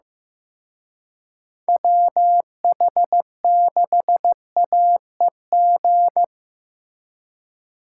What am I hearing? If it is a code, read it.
Morse code, heard as a keyed tone, about 17 wpm, 700 Hz. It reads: WH6AEG
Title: Speed Racing - USA 2x3 Call Signs 15wpm